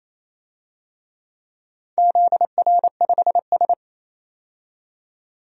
Morse code, heard as a keyed tone, about 27 wpm, 700 Hz. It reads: ZR5S
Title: DXCC-100 Call Signs - 4 Characters - 28wpm